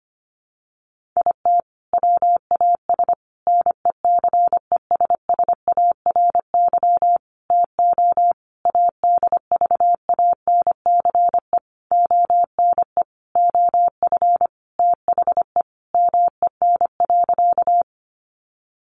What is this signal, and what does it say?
Morse code, heard as a keyed tone, about 25 wpm, 700 Hz. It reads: IT WAS NECESSARY TO ADVANCE ONE OF THE MEN.
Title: Sentences from Top 1400 Words 25wpm